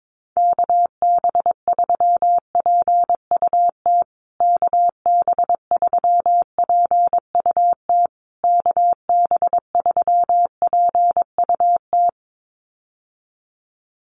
Morse code, heard as a keyed tone, about 22 wpm, 690 Hz. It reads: KB3PUT KB3PUT KB3PUT
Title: USA Call Signs Repeated 3x 22wpm